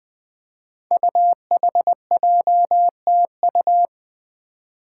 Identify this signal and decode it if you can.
Morse code, heard as a keyed tone, about 20 wpm, 700 Hz. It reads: UHJTU
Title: Five Challenging Letter-Number Combinations 20wpm